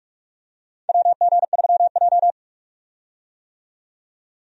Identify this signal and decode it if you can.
Morse code, heard as a keyed tone, about 46 wpm, 700 Hz. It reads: WG3J
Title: USA Call Signs - 4 Characters - 45wpm